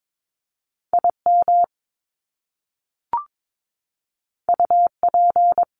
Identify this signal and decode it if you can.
Morse code, heard as a keyed tone, about 22 wpm, 700 Hz. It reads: IM UP
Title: Top Two Letter Combinations in Words 22wpm